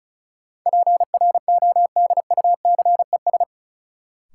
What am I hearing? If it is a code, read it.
Morse code, heard as a keyed tone, about 36 wpm, 700 Hz. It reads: PRODUCES
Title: Sentences from Top 1000 Words - Review Part 6 - New Word Drill - 35wpm